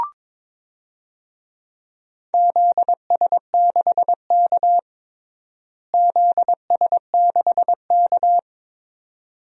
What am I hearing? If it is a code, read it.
Morse code, heard as a keyed tone, about 22 wpm, 700 Hz. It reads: ZS6K ZS6K
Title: DXCC-100 Call Signs - 4 Characters - Repeated 2x - 22wpm